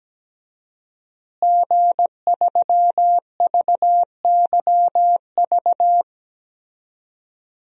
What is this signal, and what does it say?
Morse code, heard as a keyed tone, about 17 wpm, 700 Hz. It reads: G3VYV
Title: DXCC-100 Call Signs - 5 Characters - Repeated 3x - 17wpm